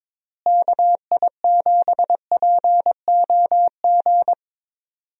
Morse code, heard as a keyed tone, about 23 wpm, 700 Hz. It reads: KI7POG
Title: USA Call Signs - 6 Characters - Repeated 3x - 22wpm